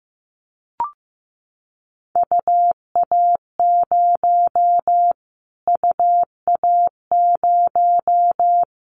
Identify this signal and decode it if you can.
Morse code, heard as a keyed tone, about 15 wpm, 700 Hz. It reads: UA0 UA0
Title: Callsign Prefixes for Top 25 DXCC - Repeated 3x - 15wpm